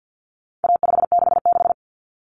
Morse code, heard as a keyed tone, about 50 wpm, 700 Hz. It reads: A566
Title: Common Cut Numbers - 4-Digit 50wpm